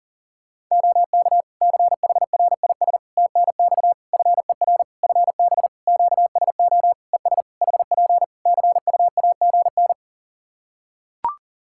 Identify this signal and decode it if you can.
Morse code, heard as a keyed tone, about 40 wpm, 700 Hz. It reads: OK CHRIS TNX FER FB QSO ES HP CUAGN
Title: Practice QSO - Exchange 4 - Ending - 40wpm